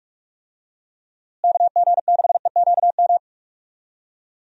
Morse code, heard as a keyed tone, about 46 wpm, 700 Hz. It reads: KG6EQM
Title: USA Call Signs - 6 Characters - Repeated 3x - 45wpm